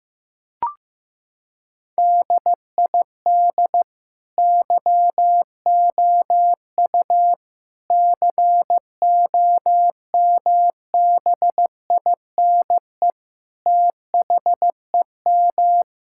Morse code, heard as a keyed tone, about 15 wpm, 700 Hz. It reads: DID YOU COMBINE THEM
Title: Sentences from Top 1600 Words 15wpm